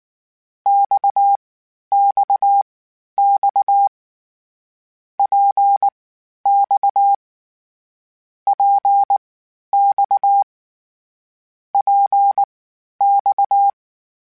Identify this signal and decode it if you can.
Morse code, heard as a keyed tone, about 19 wpm, spaced as for 11 wpm, 800 Hz. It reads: XXX PX PX PX